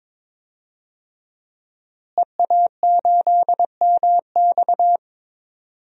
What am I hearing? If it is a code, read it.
Morse code, heard as a keyed tone, about 22 wpm, 700 Hz. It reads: EA8MX
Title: DXCC-100 Call Signs - 5 Characters - Repeated 3x - 22wpm